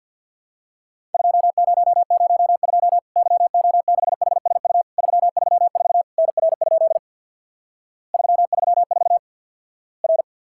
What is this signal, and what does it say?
Morse code, heard as a keyed tone, about 50 wpm, 700 Hz. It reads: 2002 YO6HSU 334 NR? 334 R
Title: Worked All Europe DX Contest - QTC Exchange with Repeats - 50wpm